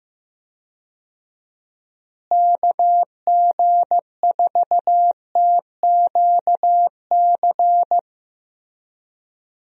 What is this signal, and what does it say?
Morse code, heard as a keyed tone, about 15 wpm, 700 Hz. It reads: KG4TQC
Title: USA Call Signs - 6 Characters - Repeated 2x - 15wpm